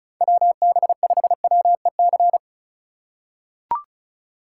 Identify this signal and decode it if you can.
Morse code, heard as a keyed tone, about 34 wpm, 700 Hz. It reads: WB5WEC
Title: K1USN SST QSO Practice - 35wpm